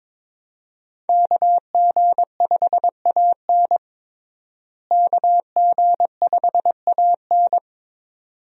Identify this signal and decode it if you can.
Morse code, heard as a keyed tone, about 22 wpm, 700 Hz. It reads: KG5AN KG5AN
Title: USA Call Signs - 5 Characters - Repeated 2x - 22wpm